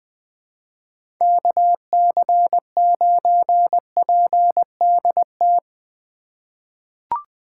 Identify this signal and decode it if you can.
Morse code, heard as a keyed tone, about 21 wpm, 700 Hz. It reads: KC9PDT
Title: USA Call Signs - 6 Characters - 4x Character Spacing - 20wpm